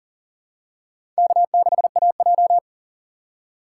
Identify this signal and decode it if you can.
Morse code, heard as a keyed tone, about 38 wpm, 700 Hz. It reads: K6AJ
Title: USA Call Signs - 4 Characters - 2x Character Spacing - 40wpm